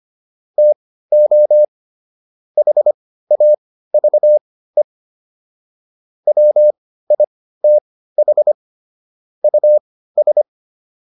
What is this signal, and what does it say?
Morse code, heard as a keyed tone, about 25 wpm, spaced as for 15 wpm, 600 Hz. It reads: TO HAVE WITH US